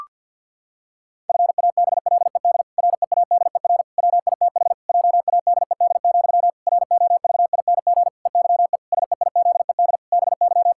Morse code, heard as a keyed tone, about 50 wpm, 700 Hz. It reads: FABLED READER WITH JADED, ROVING EYE SEIZED BY
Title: Pangram Challenge v2 - 50wpm